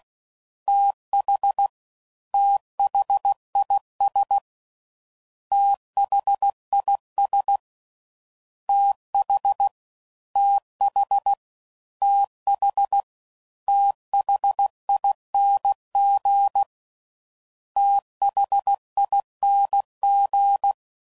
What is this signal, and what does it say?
Morse code, heard as a keyed tone, about 16 wpm, 800 Hz. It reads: TH THIS THIS TH TH TH THING THING
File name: Prefixes / FLs & THs / TH Prefixes 3